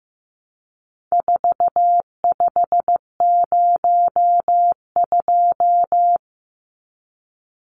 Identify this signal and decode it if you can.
Morse code, heard as a keyed tone, about 15 wpm, 700 Hz. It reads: 4502
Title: Numbers - 4-Digit 15wpm